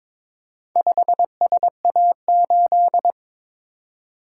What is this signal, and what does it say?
Morse code, heard as a keyed tone, about 22 wpm, 700 Hz. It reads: 5SA8
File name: practice / 22wpm / Four Letter-Number Combinations - 22wpm